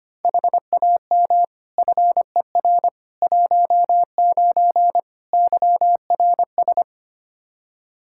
Morse code, heard as a keyed tone, about 25 wpm, 700 Hz. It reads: HAM FER 19 YRS
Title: K1USN SST QSO Practice 25wpm